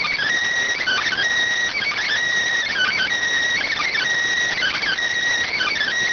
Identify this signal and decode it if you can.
Emergency Alert System (SAME) tones (header bursts), heard as harsh buzzing bursts. It is heard from 0.9 to 5.7 s.